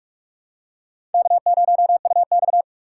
Morse code, heard as a keyed tone, about 46 wpm, 700 Hz. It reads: K0UX
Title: USA Call Signs - 4 Characters - 4x Character Spacing - 45wpm